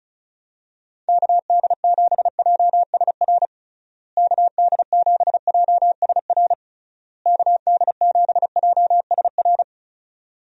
Morse code, heard as a keyed tone, about 36 wpm, 700 Hz. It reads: KD7JSR KD7JSR KD7JSR
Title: USA Call Signs - 6 Characters - Repeated 3x - 35wpm